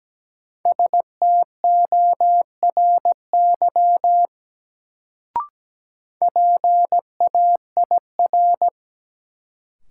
Morse code, heard as a keyed tone, about 17 wpm, 700 Hz. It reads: STORY PAIR
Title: Sets of 3 Words - New Word Drill - 17wpm